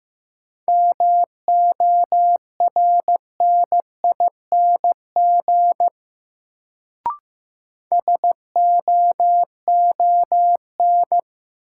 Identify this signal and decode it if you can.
Morse code, heard as a keyed tone, about 15 wpm, 700 Hz. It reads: MORNING SOON
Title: Sentences from Top 400 Words - New Word Drill - 15wpm